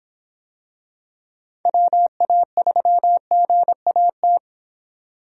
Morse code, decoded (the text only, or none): WA3GAT